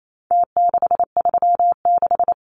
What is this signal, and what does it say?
Morse code, heard as a keyed tone, about 28 wpm, 700 Hz. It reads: T636
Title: Numbers - Contest Style - 4-Digit 28wpm